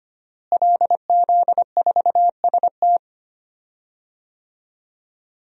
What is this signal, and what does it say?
Morse code, heard as a keyed tone, about 25 wpm, 700 Hz. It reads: LZ4ST
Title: DXCC-100 Call Signs - 5 Characters - Repeated 2x - 25wpm